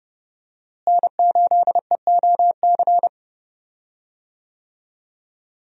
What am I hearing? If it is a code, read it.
Morse code, heard as a keyed tone, about 30 wpm, 700 Hz. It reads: N8EOC
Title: USA Call Signs - 5 Characters - Repeated 2x - 30wpm